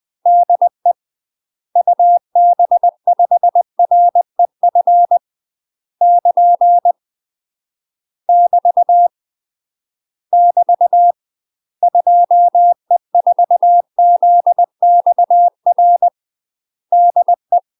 Morse code, heard as a keyed tone, about 20 wpm, 700 Hz. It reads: DE UB5REF <KN> <BT> <BT> 2E4ZXR DE
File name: UK MASTER QSOs TRAINING / 20WPM / UK MASTER QSOs 0003